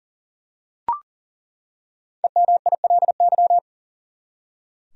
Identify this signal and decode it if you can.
Morse code, heard as a keyed tone, about 37 wpm, 700 Hz. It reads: EMILY
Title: USA Common Names - 40wpm